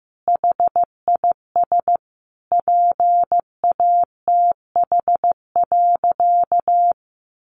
Morse code, heard as a keyed tone, about 15 wpm, 700 Hz. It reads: HIS PATH.
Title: Sentences from Top 1400 Words 15wpm